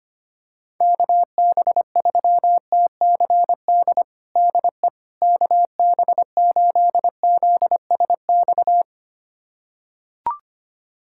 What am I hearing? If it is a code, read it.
Morse code, heard as a keyed tone, about 25 wpm, 700 Hz. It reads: KB3TCD DE KB8ZSX
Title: USA Callsign DE Callsign - 6 Characters - 25wpm